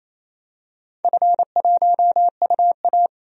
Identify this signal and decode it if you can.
Morse code, heard as a keyed tone, about 28 wpm, 700 Hz. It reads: F1UA
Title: DXCC-100 Call Signs - 4 Characters - 28wpm